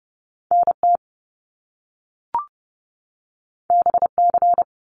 Morse code, heard as a keyed tone, about 30 wpm, 700 Hz. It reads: NT BC
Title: Canadian Provinces and Territories 30wpm